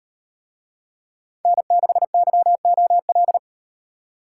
Morse code, heard as a keyed tone, about 38 wpm, 700 Hz. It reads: N6YOL